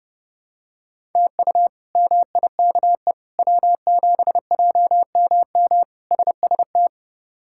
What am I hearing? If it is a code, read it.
Morse code, heard as a keyed tone, about 32 wpm, 700 Hz. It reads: TU MIKE W7JMM SST
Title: K1USN SST QSO Practice - 30wpm